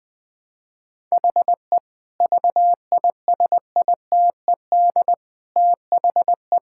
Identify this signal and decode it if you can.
Morse code, heard as a keyed tone, about 21 wpm, 700 Hz. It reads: HE VISITED THE
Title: Sentences with 4-Word Phrases - 20wpm